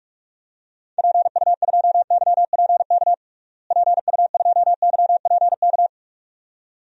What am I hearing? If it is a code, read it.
Morse code, heard as a keyed tone, about 46 wpm, 700 Hz. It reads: PU2YPK PU2YPK
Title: DXCC-100 Call Signs - 6 Characters - Repeated 2x - 45wpm